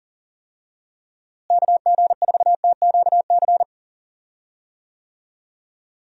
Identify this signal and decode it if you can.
Morse code, heard as a keyed tone, about 40 wpm, 700 Hz. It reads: KG4TQC
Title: USA Call Signs - 6 Characters - Repeated 2x - 40wpm